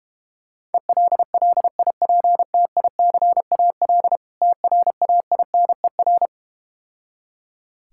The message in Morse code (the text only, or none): ELLIPTICAL TRAINER